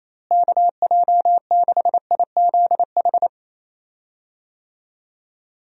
Morse code, heard as a keyed tone, about 28 wpm, 700 Hz. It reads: KJ6IZH
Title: USA Call Signs - 6 Characters - Repeated 2x - 28wpm